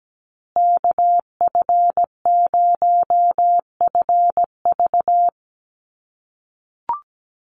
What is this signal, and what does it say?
Morse code, heard as a keyed tone, about 17 wpm, 700 Hz. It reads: KF0FV
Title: USA Call Signs Repeated 3x 17wpm